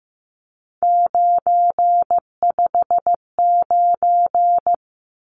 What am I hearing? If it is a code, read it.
Morse code, heard as a keyed tone, about 15 wpm, 700 Hz. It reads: 959
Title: Numbers - 3-Digit 15wpm